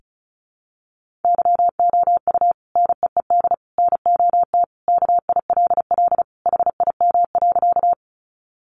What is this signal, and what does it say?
Morse code, heard as a keyed tone, about 34 wpm, 700 Hz. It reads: YOU NEED NOT KILL HIM.
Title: Sentences from Top 1400 Words 35wpm